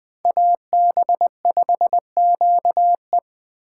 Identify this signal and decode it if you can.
Morse code, heard as a keyed tone, about 21 wpm, 700 Hz. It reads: AB5QE
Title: Speed Racing - USA 2x2 Call Signs 17wpm